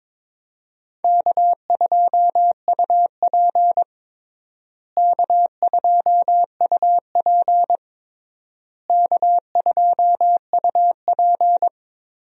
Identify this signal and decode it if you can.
Morse code, heard as a keyed tone, about 22 wpm, 700 Hz. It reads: K2UP K2UP K2UP
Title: USA Call Signs - 4 Characters - Repeated 3x - 22wpm